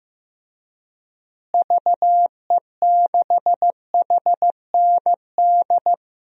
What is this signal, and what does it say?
Morse code, heard as a keyed tone, about 16 wpm, 700 Hz. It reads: VE6HND
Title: DXCC-100 Call Signs - 6 Characters - Repeated 2x - 15wpm